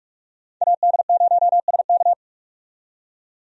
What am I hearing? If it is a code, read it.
Morse code, heard as a keyed tone, about 42 wpm, 700 Hz. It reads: AD0SK
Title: USA Call Signs - 5 Characters - 45wpm